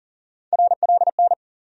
Morse code, heard as a keyed tone, about 40 wpm, 700 Hz. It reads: RLN
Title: USA Call Sign Postfixes - 3 Letters 40wpm